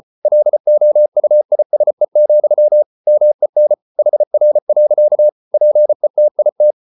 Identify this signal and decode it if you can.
Morse code, heard as a keyed tone, about 34 wpm, 600 Hz. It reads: LOUISE, MEN HR. PETIT
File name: Ferie QRQ 34wpm